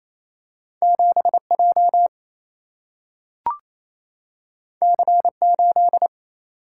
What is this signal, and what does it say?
Morse code, heard as a keyed tone, about 28 wpm, 700 Hz. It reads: 7J C8
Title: Single-Letter Single-Number 28wpm